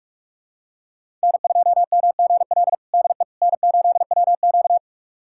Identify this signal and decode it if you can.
Morse code, heard as a keyed tone, about 46 wpm, 700 Hz. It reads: N2MGL DE N8WQ
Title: USA Callsign DE Callsign 45wpm